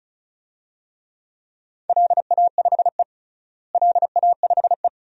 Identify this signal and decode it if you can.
Morse code, heard as a keyed tone, about 34 wpm, 700 Hz. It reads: LA5E LA5E
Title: DXCC-100 Call Signs - 4 Characters - Repeated 3x - 35wpm